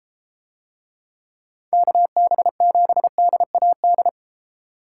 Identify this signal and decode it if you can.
Morse code, heard as a keyed tone, about 30 wpm, 700 Hz. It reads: KB7DAD